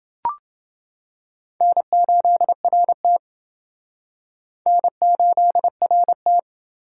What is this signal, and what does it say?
Morse code, heard as a keyed tone, about 29 wpm, 700 Hz. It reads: N8RT N8RT